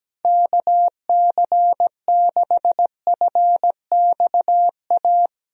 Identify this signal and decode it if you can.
Morse code, heard as a keyed tone, about 17 wpm, 700 Hz. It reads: KC6FXA
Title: USA Call Signs - 6 Characters - Repeated 2x - 17wpm